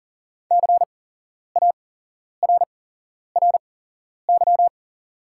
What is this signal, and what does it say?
Morse code, heard as a keyed tone, about 40 wpm, spaced as for 11 wpm, 700 Hz. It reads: CARRY
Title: 8x Character Spacing - Top 300 Words - 40wpm